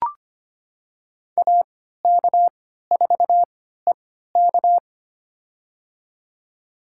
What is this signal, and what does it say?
Morse code, heard as a keyed tone, about 25 wpm, spaced as for 14 wpm, 700 Hz. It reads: AK4EK
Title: USA Call Signs - 5 Characters - 3x Character Spacing - 25wpm